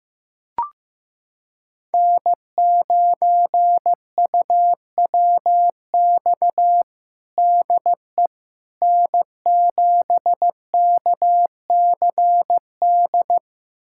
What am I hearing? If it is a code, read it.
Morse code, heard as a keyed tone, about 15 wpm, 700 Hz. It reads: N9UWX DE N7KCD
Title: USA Callsign DE Callsign - 5 Characters - 15wpm